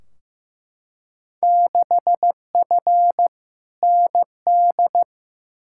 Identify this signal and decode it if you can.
Morse code, heard as a keyed tone, about 15 wpm, 700 Hz. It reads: 6F ND
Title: Field Day Exchanges Encore - 15wpm